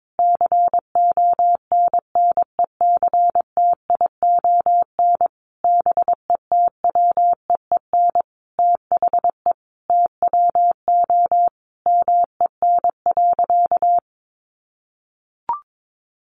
Morse code, heard as a keyed tone, about 22 wpm, 700 Hz. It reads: CONNECTION BETWEEN THE TWO MEN.
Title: Sentences from Top 1300 Words 22wpm